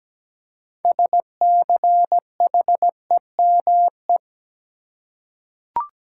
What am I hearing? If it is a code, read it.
Morse code, heard as a keyed tone, about 17 wpm, 700 Hz. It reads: SCHEME
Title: Six Letter Words - 17wpm